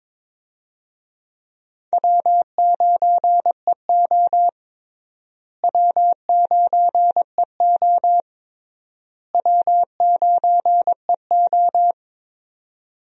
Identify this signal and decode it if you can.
Morse code, heard as a keyed tone, about 22 wpm, 700 Hz. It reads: W9EO W9EO W9EO
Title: USA Call Signs - 4 Characters - Repeated 3x - 22wpm